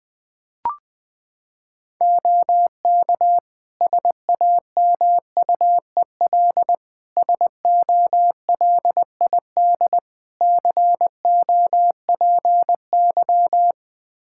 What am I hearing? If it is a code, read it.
Morse code, heard as a keyed tone, about 21 wpm, 700 Hz. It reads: OK SAMUEL SOLID COPY
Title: K1USN SST QSO Practice 20wpm